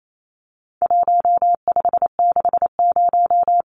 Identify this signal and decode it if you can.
Morse code, heard as a keyed tone, about 28 wpm, 700 Hz. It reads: 1560
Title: Numbers - Contest Style - 4-Digit 28wpm